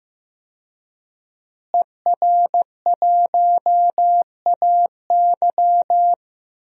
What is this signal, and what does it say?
Morse code, heard as a keyed tone, about 15 wpm, 700 Hz. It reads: ER1AY
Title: DXCC-100 Call Signs - 5 Characters - Repeated 2x - 15wpm